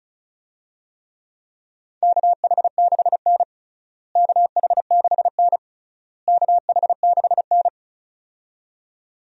Morse code, heard as a keyed tone, about 34 wpm, 700 Hz. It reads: KH6N KH6N KH6N
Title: DXCC-100 Call Signs - 4 Characters - Repeated 3x - 35wpm